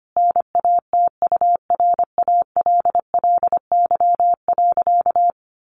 Morse code, heard as a keyed tone, about 25 wpm, 700 Hz. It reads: NATURALLY.
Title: Sentences from Top 1300 Words 25wpm